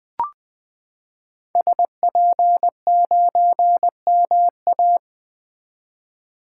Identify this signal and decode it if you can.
Morse code, heard as a keyed tone, about 20 wpm, 700 Hz. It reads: SP9MA
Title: 2023 SOTA Activator Call Signs - 20wpm